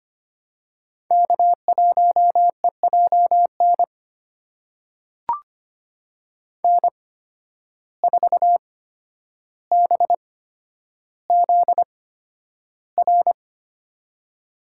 Morse code, heard as a keyed tone, about 25 wpm, spaced as for 11 wpm, 700 Hz. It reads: K1EJN N4BZR
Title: USA Call Signs - 5 Characters - 8x Character Spacing - 25wpm